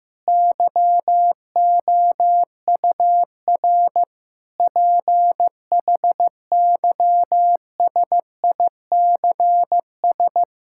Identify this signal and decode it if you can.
Morse code, heard as a keyed tone, about 15 wpm, 700 Hz. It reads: YOUR PHYSICS
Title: Sentences from Top 1000 Words - Review Part 5 - 15wpm